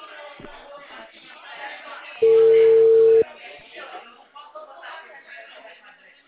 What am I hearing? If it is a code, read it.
Ringback tone (China, 450 Hz), with speech.
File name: out- -2015